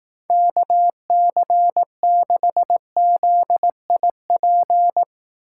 Morse code, heard as a keyed tone, about 18 wpm, 700 Hz. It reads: KC6ZIP
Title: Speed Racing - USA 2x3 Call Signs 15wpm